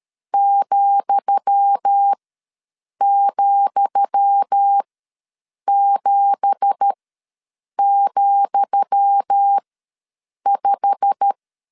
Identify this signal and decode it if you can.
Morse code, heard as a keyed tone, about 13 wpm, spaced as for 7 wpm, 800 Hz. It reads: ,,7,5
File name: LEZIONI / 10 SIGNS / 10D COMA ws 12wpm 3618